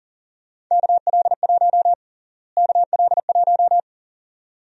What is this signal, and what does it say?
Morse code, heard as a keyed tone, about 40 wpm, 700 Hz. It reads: KL1 KL1
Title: USA Call Sign Prefixes - Repeated 2x - 40wpm